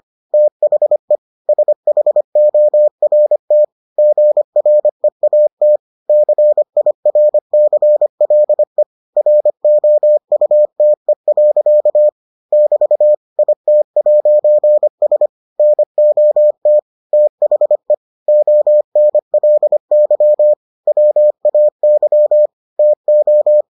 Morse code, heard as a keyed tone, about 25 wpm, 600 Hz. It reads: THE SHORT GREAT CIRCLE ROUTE. <BT> IT'S NOT THE ONLY WAY TO